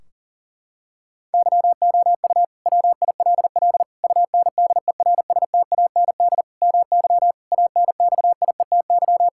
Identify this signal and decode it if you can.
Morse code, heard as a keyed tone, about 40 wpm, 700 Hz. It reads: YOU WILL UNDERSTAND MY ANXIETY
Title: Sentences from Top 1600 Words 40wpm